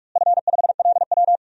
Morse code, heard as a keyed tone, about 46 wpm, 700 Hz. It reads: F5LW
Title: DXCC-100 Call Signs - 4 Characters - Repeated 3x - 45wpm